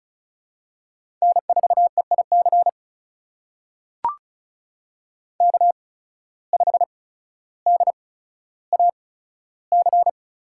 Morse code, heard as a keyed tone, about 36 wpm, spaced as for 15 wpm, 700 Hz. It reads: N4EIC K5DAC
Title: USA Call Signs - 5 Characters - 8x Character Spacing - 35wpm